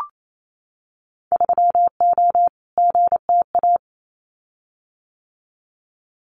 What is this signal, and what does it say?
Morse code, heard as a keyed tone, about 28 wpm, 700 Hz. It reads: 3O GTA
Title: Winter Field Day Exchanges Encore 28wpm